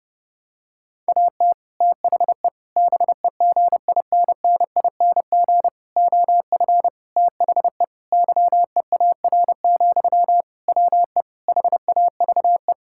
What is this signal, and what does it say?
Morse code, heard as a keyed tone, about 29 wpm, 700 Hz. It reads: AT THE BEGINNING OF THE YEAR, WE HAVE
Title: Sentences from Top 1000 Words - Review Part 5 - 30wpm